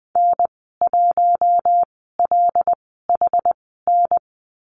Morse code, heard as a keyed tone, about 20 wpm, spaced as for 15 wpm, 700 Hz. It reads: N1LHN
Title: 179 Lesson 29g - Slash - Call Signs 20wpm